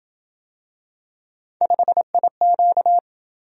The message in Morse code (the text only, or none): 5IQ